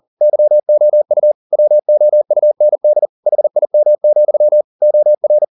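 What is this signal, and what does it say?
Morse code, heard as a keyed tone, about 40 wpm, 600 Hz. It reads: YOU WOUND HIM, OR
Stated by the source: Morse code at 40 wpm, practice text, CW med Gnister.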